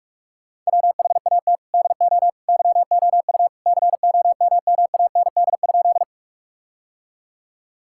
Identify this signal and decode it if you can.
Morse code, heard as a keyed tone, about 46 wpm, 700 Hz. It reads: WHAT DO YOU COMMAND?
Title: Sentences from Top 1600 Words 45wpm